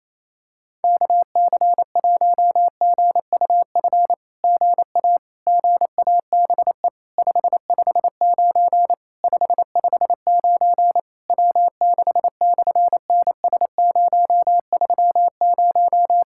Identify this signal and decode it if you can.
Morse code, heard as a keyed tone, about 28 wpm, 700 Hz. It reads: KC1GUF GA GABE 559 559 W6/NS030
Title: SOTA QSO Practice 28wpm